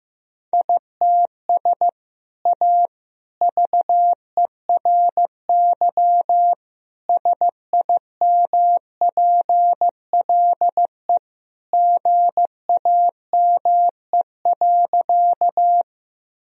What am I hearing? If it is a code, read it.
Morse code, heard as a keyed tone, about 15 wpm, 700 Hz. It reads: ITS A VERY SIMPLE GAME.
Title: Sentences from Top 1000 Words - Review Part 3 - 15wpm